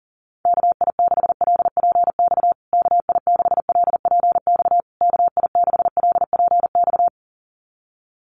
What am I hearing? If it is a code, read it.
Morse code, heard as a keyed tone, about 40 wpm, 700 Hz. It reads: KI6LPX KI6LPX KI6LPX
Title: USA Call Signs Repeated 3x 40wpm